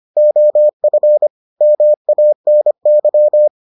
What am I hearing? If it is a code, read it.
Morse code, heard as a keyed tone, about 25 wpm, 600 Hz. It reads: OF MANY